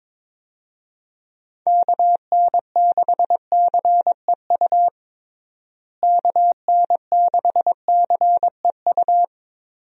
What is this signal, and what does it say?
Morse code, heard as a keyed tone, about 22 wpm, 700 Hz. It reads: KN6CEU KN6CEU
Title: USA Call Signs - 6 Characters - Repeated 2x - 22wpm